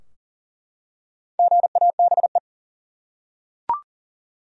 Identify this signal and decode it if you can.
Morse code, heard as a keyed tone, about 40 wpm, 700 Hz. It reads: GABE
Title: USA Common Names - 40wpm